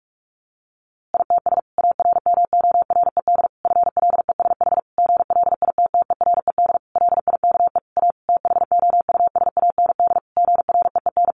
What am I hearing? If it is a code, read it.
Morse code, heard as a keyed tone, about 44 wpm, 700 Hz. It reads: ITS ARMORED FLESH GLITTERED LIKE A THOUSAND GREEN
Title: A Sound Of Thunder 45wpm